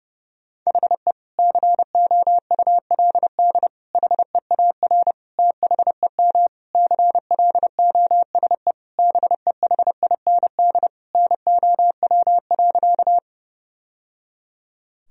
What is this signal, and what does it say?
Morse code, heard as a keyed tone, about 30 wpm, 700 Hz. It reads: HE COULD HEAR THEM CLOSE BEHIND NOW.
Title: Sentences from Top 1000 Words - Review Part 1 - 30wpm